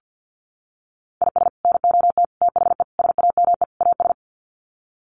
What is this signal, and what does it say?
Morse code, heard as a keyed tone, about 50 wpm, 700 Hz. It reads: IS NOT THE SAME AS
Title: Sets of 5 Words Encore 50wpm